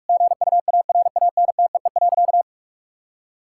Morse code, heard as a keyed tone, about 43 wpm, 700 Hz. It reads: GUARANTEE.
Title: Sentences from Top 1600 Words 45wpm